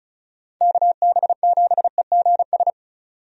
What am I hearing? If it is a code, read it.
Morse code, heard as a keyed tone, about 34 wpm, 700 Hz. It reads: KB7EGS
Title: USA Call Signs - 6 Characters - 35wpm